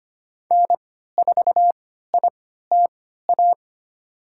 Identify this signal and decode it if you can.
Morse code, heard as a keyed tone, about 25 wpm, spaced as for 14 wpm, 700 Hz. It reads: N4ITA
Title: USA Call Signs - 5 Characters - 3x Character Spacing - 25wpm